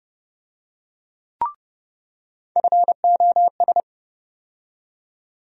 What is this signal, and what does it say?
Morse code, heard as a keyed tone, about 31 wpm, 700 Hz. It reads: FOS
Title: USA Call Sign Postfixes - 3 Letters 30wpm